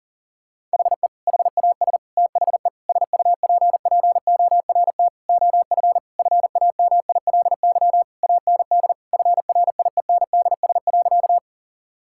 Morse code, heard as a keyed tone, about 37 wpm, 700 Hz. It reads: HE HAS THE SUPPORT OF FAMILY AND FRIENDS.
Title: Sentences from Top 1000 Words - Review Part 5 - 40wpm